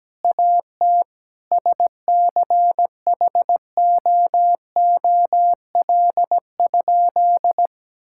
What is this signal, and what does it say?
Morse code, heard as a keyed tone, about 17 wpm, 700 Hz. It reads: AT SCHOOL?
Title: Sentences from Top 1000 Words - Review Part 6 - 17wpm